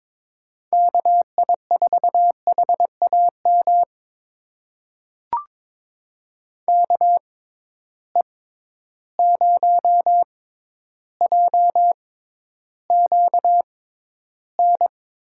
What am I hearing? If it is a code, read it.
Morse code, heard as a keyed tone, about 22 wpm, spaced as for 11 wpm, 700 Hz. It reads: KI4HAM KE0JQN
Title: USA Call Signs - 6x Character Spacing - 22wpm